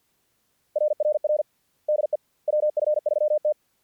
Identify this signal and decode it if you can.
Morse code, heard as a keyed tone, about 50 wpm, 600 Hz. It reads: RRR DE WF3T
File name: archived / CRSnet